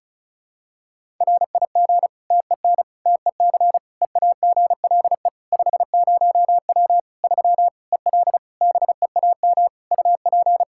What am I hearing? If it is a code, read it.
Morse code, heard as a keyed tone, about 36 wpm, 700 Hz. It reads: RIG TEN TEC EAGLE 50W 3 EL BEAM UP
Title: Practice QSO - Exchange 2 - Info - 35wpm